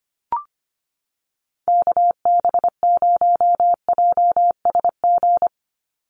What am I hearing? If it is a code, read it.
Morse code, heard as a keyed tone, about 25 wpm, 700 Hz. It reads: KB0JSG
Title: USA Call Signs 25wpm